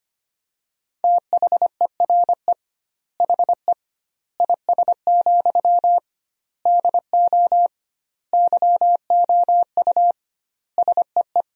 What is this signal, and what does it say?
Morse code, heard as a keyed tone, about 25 wpm, 700 Hz. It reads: THERE HE IS, DO YOU SEE
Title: 2x Word-Spacing - Sentences from Top 100 Words - 25wpm